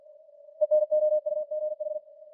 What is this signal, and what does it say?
Morse code, heard as a keyed tone, about 45 wpm, 610 Hz. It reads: ENOUGH